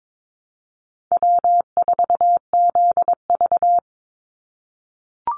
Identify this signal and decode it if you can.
Morse code, heard as a keyed tone, about 22 wpm, 700 Hz. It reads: W4ZV
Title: NA SOTA Chaser Callsigns 22wpm